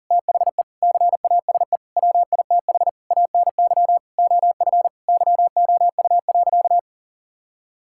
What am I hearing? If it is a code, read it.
Morse code, heard as a keyed tone, about 40 wpm, 700 Hz. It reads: THE CASE WITH ANY OF YOU.
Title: Sentences from Top 1600 Words 40wpm